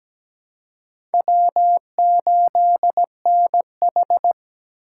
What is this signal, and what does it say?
Morse code, heard as a keyed tone, about 17 wpm, 700 Hz. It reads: W8NH
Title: USA Call Signs - 4 Characters - 4x Character Spacing - 17wpm